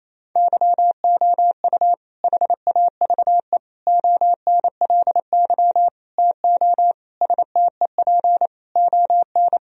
Morse code, heard as a keyed tone, about 28 wpm, 700 Hz. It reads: YOU HAVE ONLY TO STEP ON
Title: Sentences from Top 1000 Words - Review Part 4 - 28wpm